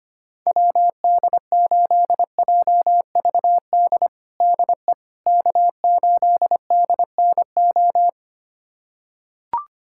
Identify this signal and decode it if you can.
Morse code, heard as a keyed tone, about 25 wpm, 700 Hz. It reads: WD8JVD DE K8DNO
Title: USA Callsign DE Callsign 25wpm